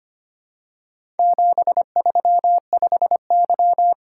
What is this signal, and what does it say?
Morse code, heard as a keyed tone, about 25 wpm, 700 Hz. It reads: 735Y
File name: practice / 25wpm / Four Letter-Number Combinations - 25wpm